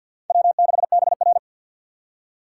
Morse code, heard as a keyed tone, about 53 wpm, 700 Hz. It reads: W6BR
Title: USA Call Signs - 2x Character Spacing - 50wpm